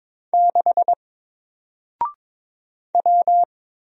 Morse code, heard as a keyed tone, about 22 wpm, 700 Hz. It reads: 6 W
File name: Single-Letter-Number-22wpm